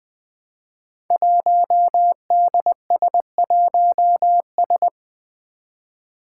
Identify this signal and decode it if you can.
Morse code, heard as a keyed tone, about 20 wpm, 700 Hz. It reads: 1DS1S
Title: Five Challenging Letter-Number Combinations 20wpm